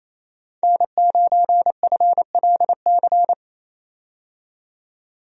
Morse code, heard as a keyed tone, about 28 wpm, 700 Hz. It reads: N9FLC
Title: USA Call Signs - 5 Characters - 28wpm